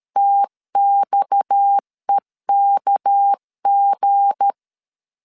Morse code, heard as a keyed tone, about 13 wpm, 800 Hz. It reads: TXEKG
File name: LEZIONI / 9 28 12wpm 2351